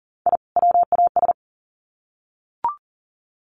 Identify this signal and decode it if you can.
Morse code, heard as a keyed tone, about 40 wpm, 700 Hz. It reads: I WAS
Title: Sets of 2 Words 40wpm